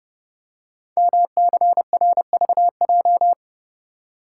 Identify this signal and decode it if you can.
Morse code, heard as a keyed tone, about 30 wpm, 700 Hz. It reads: MCRVJ
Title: Code Groups - 5 Letters - 30wpm